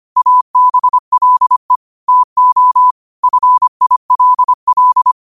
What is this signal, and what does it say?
Morse code, heard as a keyed tone, about 25 wpm, 1000 Hz. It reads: ABLE TO FILL
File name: qotdfiles / 25wpm